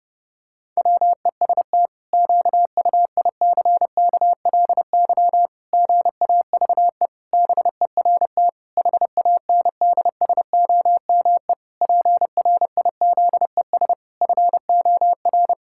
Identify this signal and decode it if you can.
Morse code, heard as a keyed tone, about 30 wpm, 700 Hz. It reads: WEST QUICKLY GAVE BERT HANDSOME PRIZES FOR
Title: Pangram Challenge v2 - 30wpm